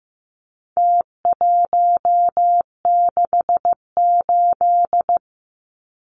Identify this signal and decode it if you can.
Morse code, heard as a keyed tone, about 15 wpm, 700 Hz. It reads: T168
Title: Numbers - Contest Style - 4-Digit 15wpm